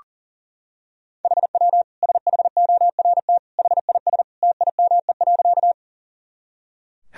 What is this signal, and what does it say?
Morse code, heard as a keyed tone, about 40 wpm, 700 Hz. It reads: HW SHORT HIS TIME.
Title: Sentences from Top 400 Words and Abbreviations 40wpm